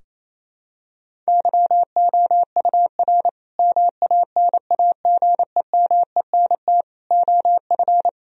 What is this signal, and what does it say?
Morse code, heard as a keyed tone, about 27 wpm, 700 Hz. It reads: YOUR MANAGEMENT OF
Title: Sentences from Top 1000 Words - Review Part 4 - 28wpm